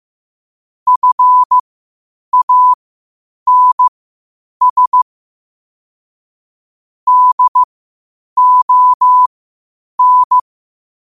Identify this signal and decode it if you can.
Morse code, heard as a keyed tone, about 15 wpm, spaced as for 8 wpm, 1000 Hz. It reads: FANS DON
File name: qotdfiles / 8wpm